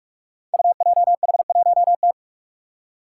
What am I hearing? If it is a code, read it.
Morse code, heard as a keyed tone, about 44 wpm, 700 Hz. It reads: UJH1T
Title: Five Challenging Letter-Number Combinations 45wpm